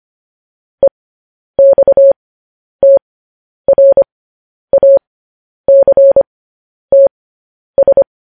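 Morse code, heard as a keyed tone, about 25 wpm, spaced as for 10 wpm, 550 Hz. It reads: EXTRACTS
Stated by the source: Morse code practice sending news headlines at 10 wpm.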